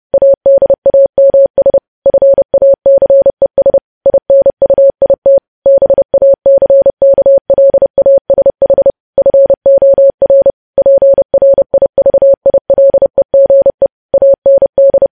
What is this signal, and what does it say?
Morse code, heard as a keyed tone, about 30 wpm, 550 Hz. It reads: ADAMS FACES INUIT BACKLASH FOR PRIVILEGE AND
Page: news headlines in morse code at 30 wpm